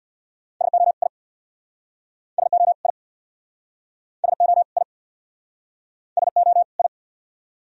Morse code, heard as a keyed tone, about 65 wpm, 700 Hz. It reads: SO I SO I SO I SO I